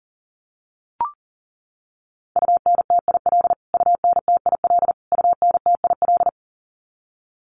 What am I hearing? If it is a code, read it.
Morse code, heard as a keyed tone, about 40 wpm, 700 Hz. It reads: UNTIL UNTIL UNTIL
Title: Top 500 Words Repeated 40wpm